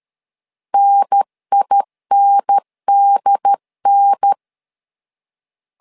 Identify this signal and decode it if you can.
Morse code, heard as a keyed tone, about 13 wpm, 800 Hz. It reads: NINDN
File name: LEZIONI / 2 UHDN / 2 UHDN 12wpm 1456